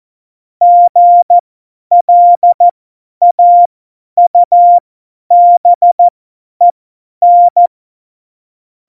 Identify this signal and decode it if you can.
Morse code, heard as a keyed tone, about 14 wpm, spaced as for 10 wpm, 700 Hz. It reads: GLAUBEN
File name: Morsekurs / Witze / Witz 26. Januar 2021, 50 F70